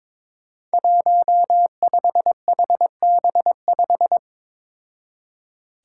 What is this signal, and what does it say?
Morse code, heard as a keyed tone, about 22 wpm, 700 Hz. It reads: 15HB5
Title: Five Challenging Letter-Number Combinations 22wpm